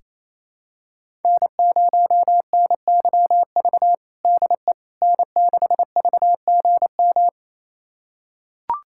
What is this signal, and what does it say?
Morse code, heard as a keyed tone, about 28 wpm, 700 Hz. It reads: N0NYV DE N6VGM
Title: USA Callsign DE Callsign - 5 Characters - 28wpm